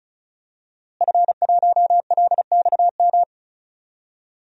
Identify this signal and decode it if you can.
Morse code, heard as a keyed tone, about 36 wpm, 700 Hz. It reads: F1LXM
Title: DXCC-100 Call Signs - 5 Characters - Repeated 2x - 35wpm